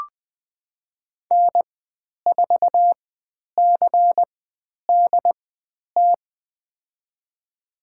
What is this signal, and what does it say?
Morse code, heard as a keyed tone, about 20 wpm, spaced as for 10 wpm, 700 Hz. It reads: N4CDT
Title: 095 Lesson 16g - Letter c - Call Signs 20wpm